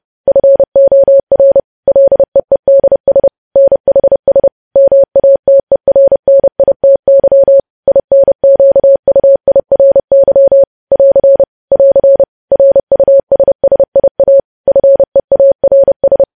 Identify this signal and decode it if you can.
Morse code, heard as a keyed tone, about 30 wpm, 550 Hz. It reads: FOR LEEDS NHS MATERNITY INQUIRY <AR> <AR> RUSSIA FEARS
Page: news headlines in morse code at 30 wpm